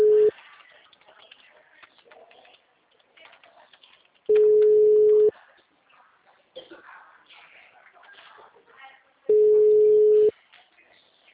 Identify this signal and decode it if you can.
Ringback tone (Europe, 425 Hz), with speech.